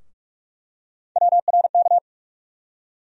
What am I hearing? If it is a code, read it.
Morse code, heard as a keyed tone, about 47 wpm, 700 Hz. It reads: WRK
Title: USA Call Sign Postfixes - 3 Letters 45wpm